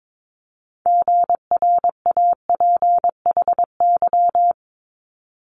Morse code, heard as a keyed tone, about 22 wpm, 700 Hz. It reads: GRAPHY
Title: Multi-letter Phonemes Quick Fire 22wpm